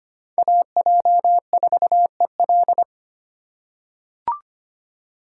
Morse code, heard as a keyed tone, about 25 wpm, 700 Hz. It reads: AJ4EL
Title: USA Call Signs - 5 Characters - 3x Character Spacing - 25wpm